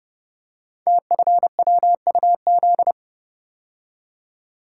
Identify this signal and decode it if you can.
Morse code, heard as a keyed tone, about 30 wpm, 700 Hz. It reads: TFWUZ
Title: Code Groups - 5 Letter:Numbers - 30wpm